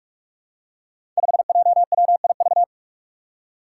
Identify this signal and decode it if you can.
Morse code, heard as a keyed tone, about 44 wpm, 700 Hz. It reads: 5JWIV
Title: Five Challenging Letter-Number Combinations 45wpm